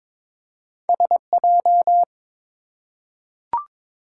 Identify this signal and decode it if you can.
Morse code, heard as a keyed tone, about 22 wpm, 700 Hz. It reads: SJ
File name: practice / 22wpm / Two Letter-Number Combinations - 22wpm